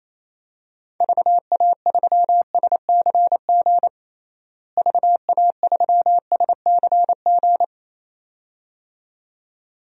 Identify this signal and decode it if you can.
Morse code, heard as a keyed tone, about 28 wpm, 700 Hz. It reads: VA3SCG VA3SCG
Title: DXCC-100 Call Signs - 6 Characters - Repeated 2x - 28wpm